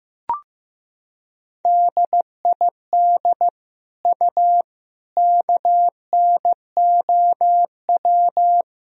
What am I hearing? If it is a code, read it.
Morse code, heard as a keyed tone, about 15 wpm, 700 Hz. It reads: DID U KNOW
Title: Sentences from Top 100 Words and Abbreviations 15wpm